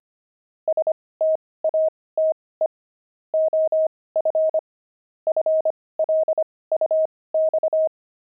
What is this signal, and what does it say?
Morse code, heard as a keyed tone, about 25 wpm, spaced as for 18 wpm, 630 Hz. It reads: STATE OF FLUX
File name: pangram-fs-18-cs-25-jelly